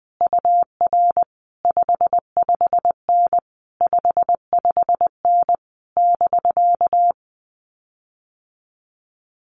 Morse code, heard as a keyed tone, about 20 wpm, 700 Hz. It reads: UR 55N 55N <BK>
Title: Common QSO Phrases Encore 20wpm